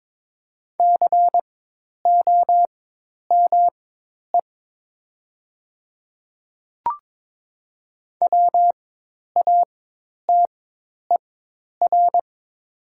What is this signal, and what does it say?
Morse code, heard as a keyed tone, about 22 wpm, spaced as for 10 wpm, 700 Hz. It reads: COME WATER
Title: 4x Character Spacing - Top 200 Words - 22wpm